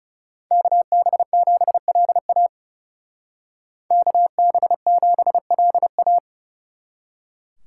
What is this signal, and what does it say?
Morse code, heard as a keyed tone, about 39 wpm, 700 Hz. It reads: KB7LA KB7LA